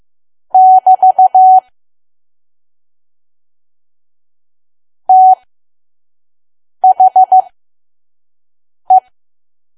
Morse code, heard as a keyed tone, about 15 wpm, spaced as for 5 wpm, 750 Hz. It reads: <BT> THE
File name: Archive / 5 WPM files / PM